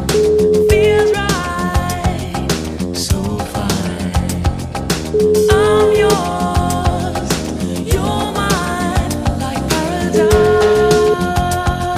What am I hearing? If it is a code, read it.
Ringback tone (Europe, 425 Hz), with noise.